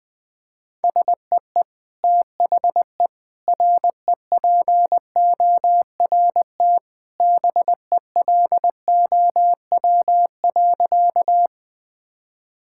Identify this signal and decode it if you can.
Morse code, heard as a keyed tone, about 20 wpm, 700 Hz. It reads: SEE THE REPORT BELOW.
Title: Sentences from Top 1000 Words - Review Part 6 - 20wpm